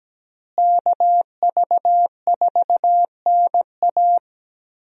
Morse code, heard as a keyed tone, about 17 wpm, 700 Hz. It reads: KV4NA
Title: USA Call Signs - 5 Characters - Repeated 2x - 17wpm